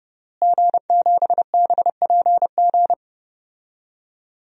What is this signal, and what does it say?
Morse code, heard as a keyed tone, about 30 wpm, 700 Hz. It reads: G7BPG
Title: DXCC-100 Call Signs - 5 Characters - Repeated 2x - 30wpm